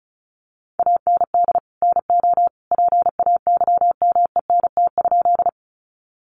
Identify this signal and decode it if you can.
Morse code, heard as a keyed tone, about 34 wpm, 700 Hz. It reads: AND NO PAYMENT?
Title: Sentences from Top 1200 Words 35wpm